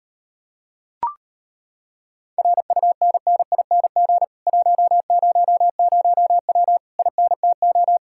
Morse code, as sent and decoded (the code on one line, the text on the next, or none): .-. ..- -. -. .. -. --. / .---- ----- ----- .-- / .. -. - ---
RUNNING 100W INTO